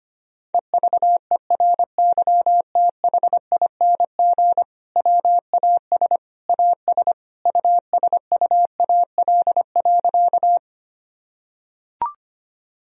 Morse code, heard as a keyed tone, about 25 wpm, 700 Hz. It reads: EVERYTHING WAS AS USUAL.
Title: Sentences from Top 1000 Words - Review Part 2 - 25wpm